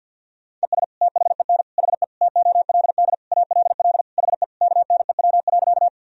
Morse code, heard as a keyed tone, about 51 wpm, 700 Hz. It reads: ES THEN HE TOLD ALL HE KNEW.
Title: Sentences from Top 400 Words and Abbreviations 50wpm